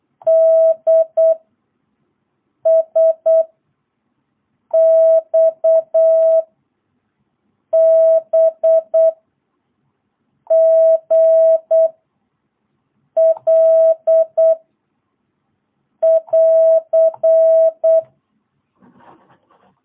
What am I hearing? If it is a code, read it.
Morse code, heard as a keyed tone, about 8 wpm, 660 Hz. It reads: D S X B G L <AR>
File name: jh8cbh / 2022cw